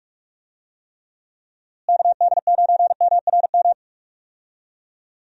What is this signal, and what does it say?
Morse code, heard as a keyed tone, about 43 wpm, 700 Hz. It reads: KD9MRM
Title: USA Call Signs - 6 Characters - Repeated 2x - 45wpm